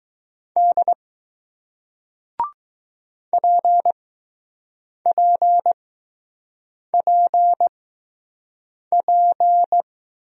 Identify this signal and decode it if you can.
Morse code, heard as a keyed tone, about 20 wpm, spaced as for 6 wpm, 700 Hz. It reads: D PPPP